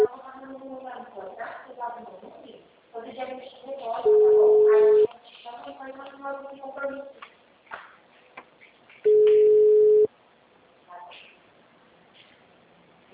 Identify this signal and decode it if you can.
Ringback tone (Europe, 425 Hz), with speech.
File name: out- -2006- - - .11835